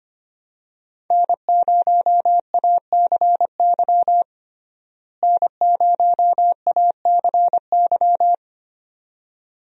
Morse code, heard as a keyed tone, about 25 wpm, 700 Hz. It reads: N0ACY N0ACY
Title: USA Call Signs - 5 Characters - Repeated 2x - 25wpm